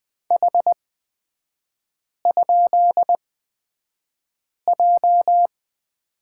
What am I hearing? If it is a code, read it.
Morse code, heard as a keyed tone, about 20 wpm, spaced as for 5 wpm, 700 Hz. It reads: H?J
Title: 243 Lesson 39e - Letter z - Three Characters 20wpm